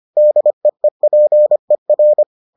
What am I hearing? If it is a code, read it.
Morse code, heard as a keyed tone, about 25 wpm, 600 Hz. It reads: DEEPER